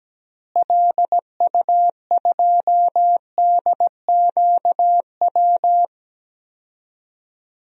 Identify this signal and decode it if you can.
Morse code, heard as a keyed tone, about 17 wpm, 700 Hz. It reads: LU2DQW
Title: DXCC-100 Call Signs - 6 Characters - Repeated 3x - 17wpm